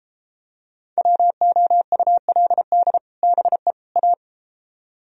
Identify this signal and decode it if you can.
Morse code, heard as a keyed tone, about 33 wpm, 700 Hz. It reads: WOULD BE A